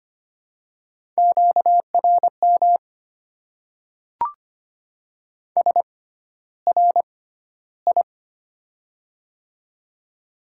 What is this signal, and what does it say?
Morse code, heard as a keyed tone, about 25 wpm, spaced as for 13 wpm, 700 Hz. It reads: QRM SRI
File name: Top 90 QSO Elements - 6x Character Spacing - 25wpm